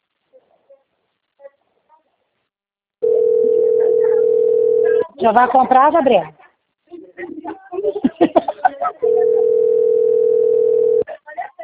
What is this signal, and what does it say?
Ringback tone (North America), 440 + 480 Hz, with speech.
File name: rg-1600-unknown- - - .71340